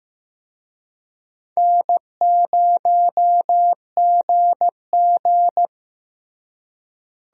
Morse code, heard as a keyed tone, about 15 wpm, 700 Hz. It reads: N0GG
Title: USA Call Signs - 4 Characters - Repeated 3x - 15wpm